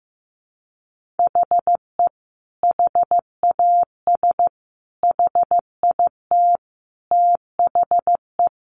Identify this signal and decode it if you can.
Morse code, heard as a keyed tone, about 15 wpm, 700 Hz. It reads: HE HAS HIT THE
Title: Sentences from Top 1200 Words 15wpm